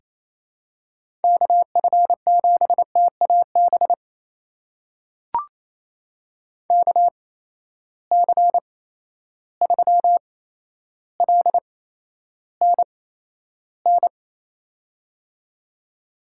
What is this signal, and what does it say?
Morse code, heard as a keyed tone, about 28 wpm, spaced as for 12 wpm, 700 Hz. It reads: KF7TAB KC3LNN
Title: USA Call Signs - 6 Characters - 8x Character Spacing - 28wpm